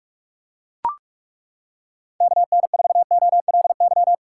Morse code, heard as a keyed tone, about 36 wpm, 700 Hz. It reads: KN4OLY
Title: Speed Racing - USA 2x3 Call Signs 30wpm